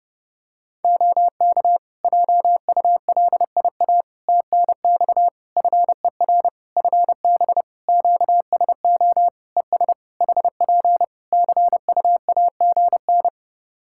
Morse code, heard as a keyed tone, about 28 wpm, 700 Hz. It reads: OK JULIA TNX FER FB QSO ES HP CUAGN
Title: Practice QSO - Exchange 4 - Ending - 30wpm